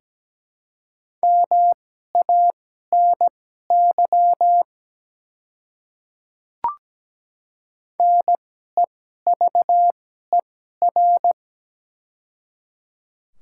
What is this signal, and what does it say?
Morse code, heard as a keyed tone, about 17 wpm, spaced as for 12 wpm, 700 Hz. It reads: MANY NEVER
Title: 2x Character Spacing - Top 300 Words - 17wpm